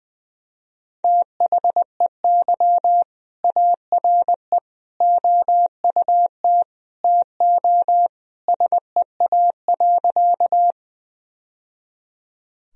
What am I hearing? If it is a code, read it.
Morse code, heard as a keyed tone, about 20 wpm, 700 Hz. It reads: THEY ARE OUT TO SEA.
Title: Sentences from Top 1000 Words - Review Part 1 - 20wpm